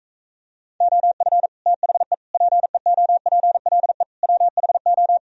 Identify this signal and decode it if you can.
Morse code, heard as a keyed tone, about 44 wpm, 700 Hz. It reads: OF THE PEOPLE WHO